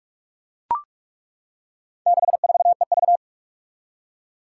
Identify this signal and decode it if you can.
Morse code, heard as a keyed tone, about 41 wpm, 700 Hz. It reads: 64EV
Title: Four Challenging Letter-Number Combinations 45wpm